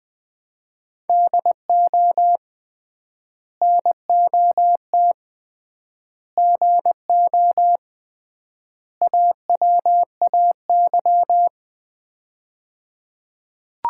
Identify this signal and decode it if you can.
Morse code, heard as a keyed tone, about 21 wpm, 700 Hz. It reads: DO NOT GO AWAY
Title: 3x Word-Spacing - Sets of 4 Words - 20wpm